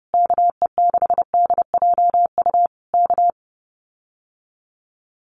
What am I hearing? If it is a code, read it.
Morse code, heard as a keyed tone, about 30 wpm, 700 Hz. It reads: KE6DJU K
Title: Common QSO Phrases Encore 30wpm